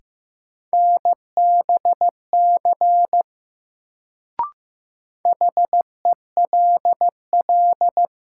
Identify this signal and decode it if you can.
Morse code, heard as a keyed tone, about 15 wpm, 700 Hz. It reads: NBC HELL
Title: Sets of 5 Words Encore - New Word Drill - 15wpm